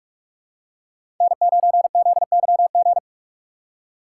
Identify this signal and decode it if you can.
Morse code, heard as a keyed tone, about 46 wpm, 700 Hz. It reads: N9ZYG
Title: USA Call Signs - 5 Characters - 3x Character Spacing - 45wpm